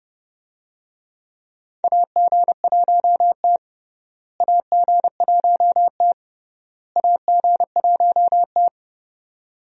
Morse code, heard as a keyed tone, about 30 wpm, 700 Hz. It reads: AG1T AG1T AG1T
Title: USA Call Signs - 4 Characters - Repeated 3x - 30wpm